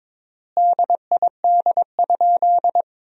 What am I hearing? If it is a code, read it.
Morse code, heard as a keyed tone, about 22 wpm, 700 Hz. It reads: DID?
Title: 3x Word-Spacing - Sentences from Top 100 Words - 22wpm